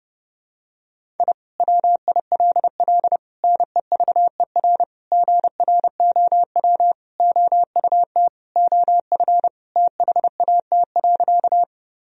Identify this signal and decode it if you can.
Morse code, heard as a keyed tone, about 30 wpm, 700 Hz. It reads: I WILL NEVER GROW OUT OF THAT.
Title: Sentences from Top 1000 Words - Review Part 1 - 30wpm